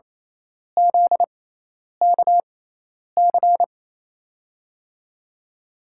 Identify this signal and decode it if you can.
Morse code, heard as a keyed tone, about 27 wpm, spaced as for 10 wpm, 700 Hz. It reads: ZKC
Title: USA Call Signs - 5 Characters - 6x Character Spacing - 28wpm